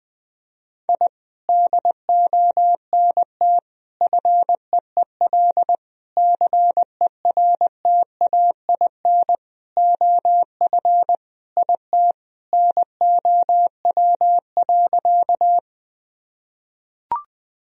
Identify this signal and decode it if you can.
Morse code, heard as a keyed tone, about 20 wpm, 700 Hz. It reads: I DONT FEEL CERTAIN OF IT NOW.
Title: Sentences from Top 1000 Words - Review Part 2 - 20wpm